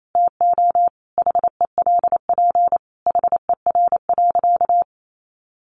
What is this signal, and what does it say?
Morse code, heard as a keyed tone, about 28 wpm, 700 Hz. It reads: TO HELP HER.
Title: Sentences from Top 1000 Words 28wpm